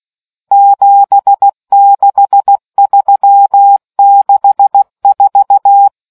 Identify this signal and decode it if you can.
Morse code, heard as a keyed tone, about 16 wpm, 800 Hz. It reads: 76364